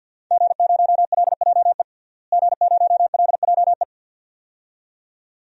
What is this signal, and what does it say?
Morse code, heard as a keyed tone, about 46 wpm, 700 Hz. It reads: G0LJE G0LJE
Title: DXCC-100 Call Signs - 5 Characters - Repeated 2x - 50wpm